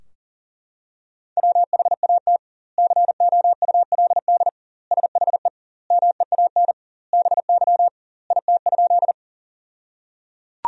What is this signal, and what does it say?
Morse code, heard as a keyed tone, about 37 wpm, 700 Hz. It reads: WHAT COULD SHE MEAN BY IT?
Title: 2x Word-Spacing - Sentences from Top 200 Words - 40wpm